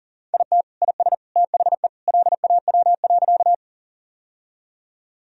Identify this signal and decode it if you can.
Morse code, heard as a keyed tone, about 40 wpm, 700 Hz. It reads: IT IS THE LAW.
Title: Sentences from Top 1000 Words - Review Part 3 - 40wpm